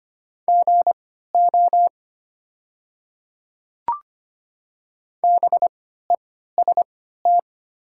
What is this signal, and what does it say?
Morse code, heard as a keyed tone, about 25 wpm, spaced as for 14 wpm, 690 Hz. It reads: GO BEST
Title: 3x Character Spacing - Top 500 Words - 25wpm